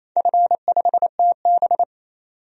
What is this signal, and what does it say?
Morse code, heard as a keyed tone, about 27 wpm, 700 Hz. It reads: F5TB
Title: DXCC-100 Call Signs - 4 Characters - 28wpm